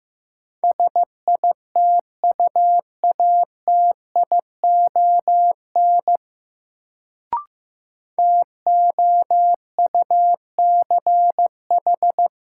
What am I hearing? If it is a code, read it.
Morse code, heard as a keyed tone, about 15 wpm, 700 Hz. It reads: SITUATION TOUCH
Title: Sentences from Top 1200 Words - New Word Drill - 15wpm